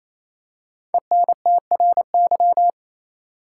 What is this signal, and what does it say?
Morse code, heard as a keyed tone, about 28 wpm, 700 Hz. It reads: ENTRY
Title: Five Letter Words - 28wpm